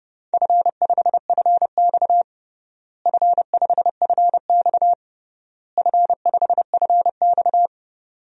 Morse code, heard as a keyed tone, about 30 wpm, 700 Hz. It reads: F5FX F5FX F5FX
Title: DXCC-100 Call Signs - 4 Characters - Repeated 3x - 30wpm